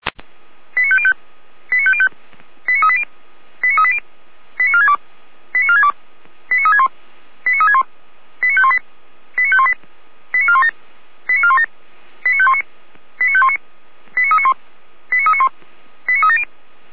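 Five-tone selective call (ZVEI), heard as quick runs of five tones, tones of about 70 ms. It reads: 89505 89505 89380 89380 89462 89462 89361 89361 89518 89518 89417 89417 89519 89519 89391 89391 89370
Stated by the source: ZVEI five-tone selective call received by SDR.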